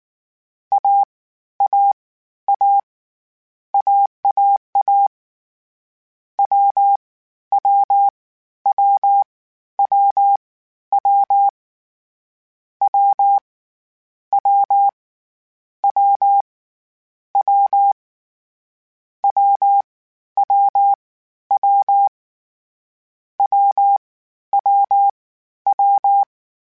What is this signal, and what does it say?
Morse code, heard as a keyed tone, about 19 wpm, 800 Hz. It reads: A A A AAA W W W W W W W W W W W W W W W
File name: Difficult Combos / Difficult Combos-mp3s / AWJ1 @18